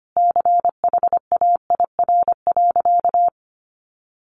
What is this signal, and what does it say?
Morse code, heard as a keyed tone, about 25 wpm, 700 Hz. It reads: CHAIR.
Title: A Sound Of Thunder 25wpm